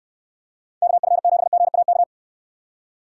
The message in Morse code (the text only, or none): KF7GAK